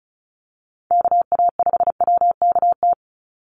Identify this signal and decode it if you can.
Morse code, heard as a keyed tone, about 34 wpm, 700 Hz. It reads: KA5WKT
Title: USA Call Signs 35wpm